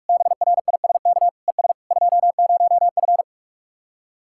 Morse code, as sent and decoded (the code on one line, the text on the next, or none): -... .-. .. ... -.- / . ... / ..--- ----- ..-.
BRISK ES 20F